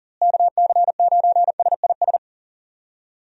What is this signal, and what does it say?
Morse code, heard as a keyed tone, about 40 wpm, 700 Hz. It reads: KC9SIS
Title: USA Call Signs - 6 Characters - Repeated 2x - 40wpm